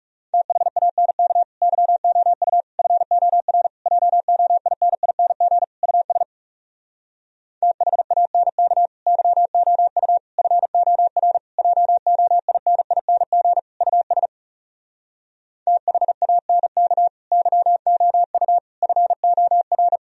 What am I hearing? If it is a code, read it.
Morse code, heard as a keyed tone, about 40 wpm, 700 Hz. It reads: THANK YOU FOR JOINING US THANK YOU FOR JOINING US THANK YOU FOR